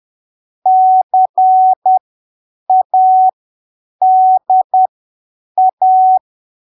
Morse code, heard as a keyed tone, about 10 wpm, spaced as for 7 wpm, 750 Hz. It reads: CADA
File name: arquivos / texto / texto 24 10ppm